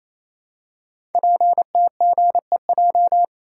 Morse code, heard as a keyed tone, about 29 wpm, 700 Hz. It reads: PTGEJ
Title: Code Groups - 5 Letters - 28wpm